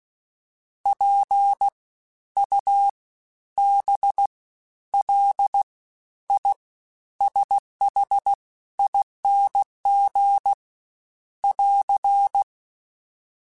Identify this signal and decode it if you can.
Morse code, heard as a keyed tone, about 16 wpm, spaced as for 11 wpm, 800 Hz. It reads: PUBLISHING <AR>
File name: Special INGs-mp3s / Special INGs-mp3s / SlowToFaster ing @15